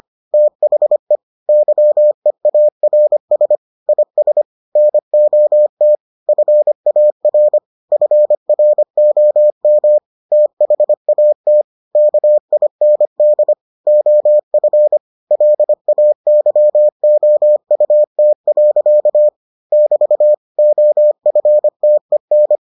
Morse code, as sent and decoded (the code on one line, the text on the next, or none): - .... / -.-- . .- .-. ... / .. ... / -. --- - / ..-. .- .-. / ..-. .-. --- -- / - .... .- - / -.- .. -. -.. / --- ..-. / .-.. .- -.-- --- ..- - .-.-.- / -...- / --- ..-. - . -.
THE YEARS IS NOT FAR FROM THAT KIND OF LAYOUT. <BT> OFTEN